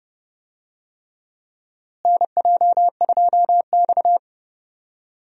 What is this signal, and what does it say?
Morse code, heard as a keyed tone, about 30 wpm, 700 Hz. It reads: NJ2X
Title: USA Call Signs - 4 Characters - Repeated 3x - 30wpm